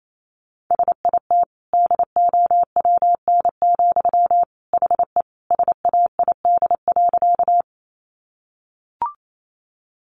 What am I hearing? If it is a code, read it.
Morse code, heard as a keyed tone, about 28 wpm, 700 Hz. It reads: SIT DOWN, HE SAID.
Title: Sentences from Top 500 Words 28wpm